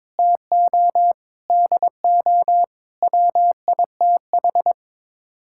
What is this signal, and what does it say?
Morse code, heard as a keyed tone, about 22 wpm, 700 Hz. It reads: TO DO WITH
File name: practice / 22wpm / Sets of 8 Words - 22wpm